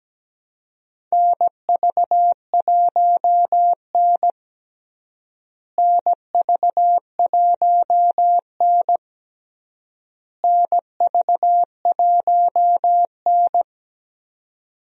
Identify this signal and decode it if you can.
Morse code, heard as a keyed tone, about 17 wpm, 700 Hz. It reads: NV1N NV1N NV1N
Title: USA Call Signs - 4 Characters - Repeated 3x - 17wpm